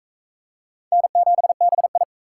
Morse code, heard as a keyed tone, about 44 wpm, 700 Hz. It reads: N7BI